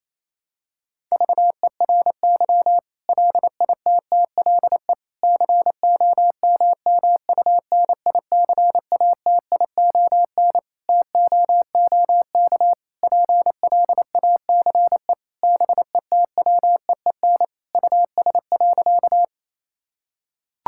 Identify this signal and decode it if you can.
Morse code, heard as a keyed tone, about 28 wpm, 700 Hz. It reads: VERY LITTLE COMMUNICATION TOOK PLACE BETWEEN US.
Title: Sentences from Top 1000 Words - Review Part 6 - 28wpm